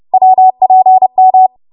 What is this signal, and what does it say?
Morse code, heard as a keyed tone, about 30 wpm, 750 Hz. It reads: WPM